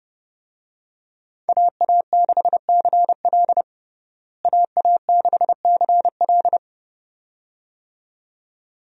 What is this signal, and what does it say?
Morse code, heard as a keyed tone, about 30 wpm, 700 Hz. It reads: AA6CL AA6CL
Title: USA Call Signs - 5 Characters - Repeated 2x - 30wpm